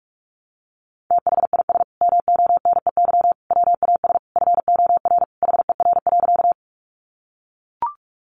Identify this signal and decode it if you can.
Morse code, heard as a keyed tone, about 44 wpm, 700 Hz. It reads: THIS MONEY WAS FOR HER.
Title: Sentences from Top 400 Words 45wpm